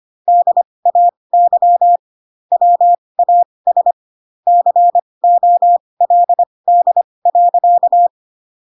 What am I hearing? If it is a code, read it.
Morse code, heard as a keyed tone, about 25 wpm, spaced as for 20 wpm, 700 Hz. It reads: DAY WAS COLD.
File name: Pinocchio / Pinocchio ch8 15 25 ik1hge